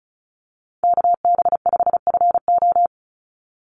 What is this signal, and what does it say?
Morse code, heard as a keyed tone, about 34 wpm, 700 Hz. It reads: KB5FO
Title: USA Call Signs 35wpm